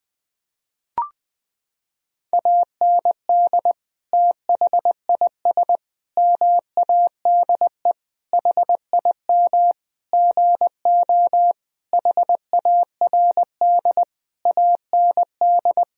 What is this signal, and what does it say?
Morse code, heard as a keyed tone, about 21 wpm, 700 Hz. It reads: AND THIS MADE HIM GO HARD AND
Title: Sentences from Top 1000 Words - Review Part 1 - 20wpm